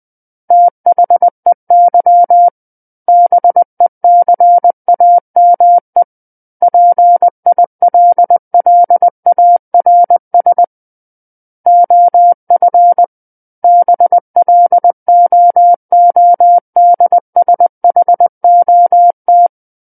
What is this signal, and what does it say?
Morse code, heard as a keyed tone, about 20 wpm, 700 Hz. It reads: THEY BECAME PILLARS OF BLOODSHOT
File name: 11-Chapter 11 - 20 WPM